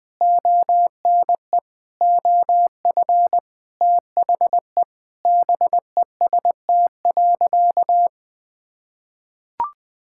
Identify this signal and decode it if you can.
Morse code, heard as a keyed tone, about 20 wpm, 700 Hz. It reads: ONE OF THE BEST.
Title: Sentences with 7-Word Phrases - Spoken Beginning - 20wpm